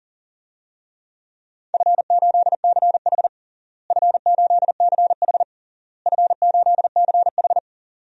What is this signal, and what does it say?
Morse code, heard as a keyed tone, about 37 wpm, 700 Hz. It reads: F8CH F8CH F8CH
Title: DXCC-100 Call Signs - 4 Characters - Repeated 3x - 40wpm